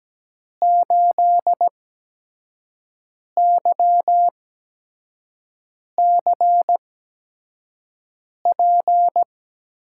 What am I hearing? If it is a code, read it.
Morse code, heard as a keyed tone, about 17 wpm, spaced as for 5 wpm, 700 Hz. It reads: 8YCP
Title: USA Call Signs - 6 Characters - 8x Character Spacing - 17wpm